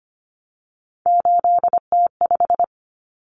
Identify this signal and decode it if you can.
Morse code, heard as a keyed tone, about 25 wpm, 700 Hz. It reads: 8T5
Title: Common Cut Numbers - 3-Digit 25wpm